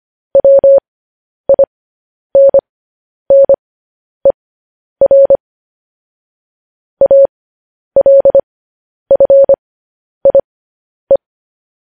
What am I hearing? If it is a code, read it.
Morse code, heard as a keyed tone, about 25 wpm, spaced as for 10 wpm, 550 Hz. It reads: WINNER ALFIE
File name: podcast / News headlines-1x-25x10